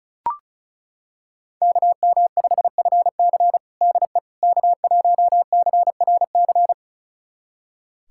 Morse code, heard as a keyed tone, about 36 wpm, 700 Hz. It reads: KM5FC DE K1CRC
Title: USA Callsign DE Callsign - 5 Characters - 35wpm